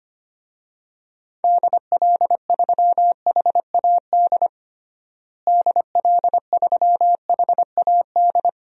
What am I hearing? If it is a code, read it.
Morse code, heard as a keyed tone, about 25 wpm, 700 Hz. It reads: DL3HAD DL3HAD
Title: DXCC-100 Call Signs - 6 Characters - Repeated 2x - 25wpm